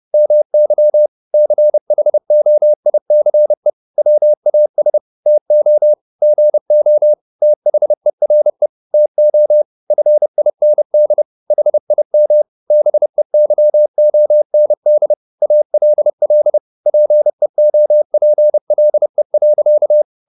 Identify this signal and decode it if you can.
Morse code, heard as a keyed tone, about 30 wpm, 600 Hz. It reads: MY CHOICE WAS TO GO THERE TO FIND HIM BEYOND ALL PEOPLE.
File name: the old man and the sea / the old man and the sea 1 30wpm